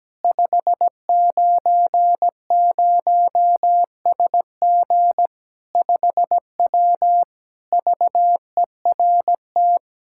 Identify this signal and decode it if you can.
Morse code, heard as a keyed tone, about 17 wpm, 700 Hz. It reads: 590SG 5W VERT
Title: Practice QSO - Exchange 2 - Info - 17wpm